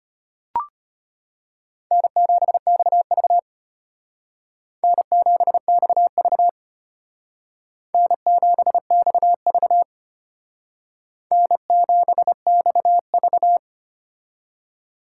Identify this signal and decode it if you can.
Morse code, heard as a keyed tone, about 32 wpm, 700 Hz. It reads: N7XV N7XV N7XV N7XV